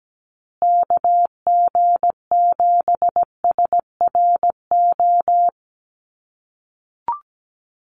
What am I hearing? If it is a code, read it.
Morse code, heard as a keyed tone, about 17 wpm, 700 Hz. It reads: KG7SRO
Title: USA Call Signs Repeated 3x 17wpm